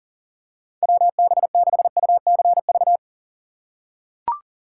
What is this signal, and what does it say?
Morse code, heard as a keyed tone, about 37 wpm, 700 Hz. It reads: WB6UCV
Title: USA Call Signs - 2x Character Spacing - 40wpm